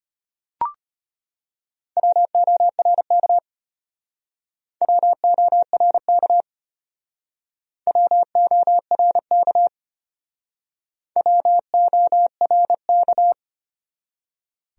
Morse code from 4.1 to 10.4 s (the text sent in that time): WORK WORK